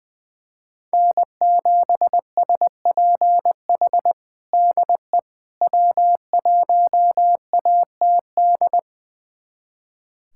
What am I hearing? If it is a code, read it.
Morse code, heard as a keyed tone, about 20 wpm, 700 Hz. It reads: N7SPH DE W1ATD
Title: USA Callsign DE Callsign - 5 Characters - 20wpm